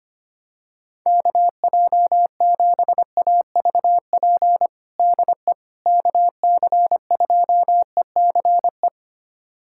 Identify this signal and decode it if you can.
Morse code, heard as a keyed tone, about 25 wpm, 700 Hz. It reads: KJ7AVP DE KC2ECE
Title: USA Callsign DE Callsign 25wpm